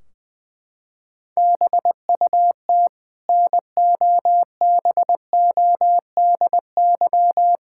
Morse code, heard as a keyed tone, about 20 wpm, 700 Hz. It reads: BUT NOBODY
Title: Sentences from Top 1600 Words 20wpm